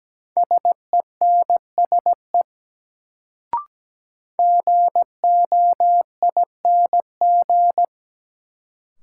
Morse code, heard as a keyed tone, about 17 wpm, 700 Hz. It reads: SENSE GOING
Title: Sets of 2 Words - New Word Drill - 17wpm